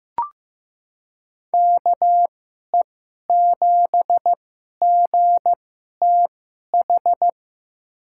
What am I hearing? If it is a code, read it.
Morse code, heard as a keyed tone, about 15 wpm, spaced as for 11 wpm, 700 Hz. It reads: KE7GTH
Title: USA Call Signs - 6 Characters - 2x Character Spacing - 15wpm